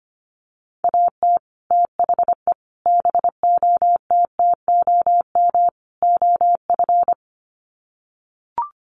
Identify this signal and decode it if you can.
Morse code, heard as a keyed tone, about 25 wpm, 700 Hz. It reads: AT THE BOTTOM OF
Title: Sets of 4 Words 25wpm